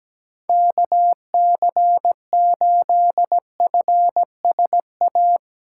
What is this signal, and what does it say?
Morse code, heard as a keyed tone, about 17 wpm, 700 Hz. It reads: KC8FSA
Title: USA Call Signs - 6 Characters - 8x Character Spacing - 17wpm